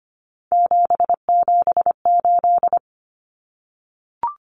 Morse code, heard as a keyed tone, about 25 wpm, 700 Hz. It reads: 778
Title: Common Cut Numbers - 3-Digit 25wpm